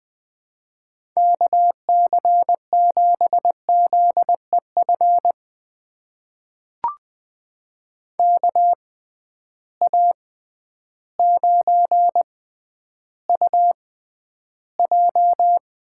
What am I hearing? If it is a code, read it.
Morse code, heard as a keyed tone, about 20 wpm, spaced as for 11 wpm, 700 Hz. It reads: KC7ZEF KA9UJ
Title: USA Call Signs - 6 Characters - 6x Character Spacing - 20wpm